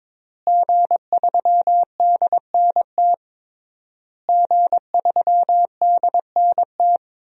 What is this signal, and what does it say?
Morse code, heard as a keyed tone, about 22 wpm, 700 Hz. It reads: G3DNT G3DNT
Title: DXCC-100 Call Signs - 5 Characters - Repeated 3x - 22wpm